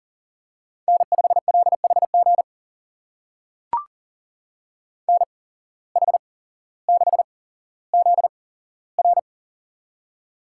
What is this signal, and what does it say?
Morse code, heard as a keyed tone, about 42 wpm, spaced as for 17 wpm, 700 Hz. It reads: N5LHG NH6ZR
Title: USA Call Signs - 5 Characters - 8x Character Spacing - 40wpm